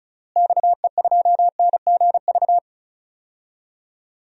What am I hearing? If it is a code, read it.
Morse code, heard as a keyed tone, about 36 wpm, 700 Hz. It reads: XE2NGV
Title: DXCC-100 Call Signs - 6 Characters - Repeated 3x - 35wpm